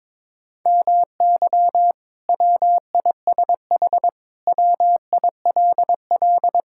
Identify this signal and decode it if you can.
Morse code, heard as a keyed tone, about 22 wpm, 700 Hz. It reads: MY WISH WILL
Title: Sentences from Top 1600 Words 22wpm